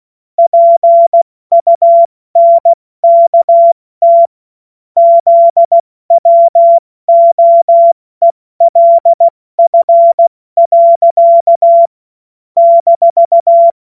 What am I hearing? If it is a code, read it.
Morse code, heard as a keyed tone, about 16 wpm, 680 Hz. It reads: PUNKT ZWOELF. -
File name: Morsekurs / Witze / Witz 24 Juni Mai 2025 70 F80